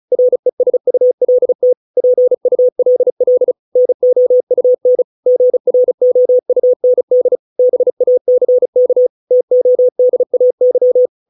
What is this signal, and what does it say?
Morse code, heard as a keyed tone, about 34 wpm, 500 Hz. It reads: RESULT PULL NOUN GROUND BACK TODAY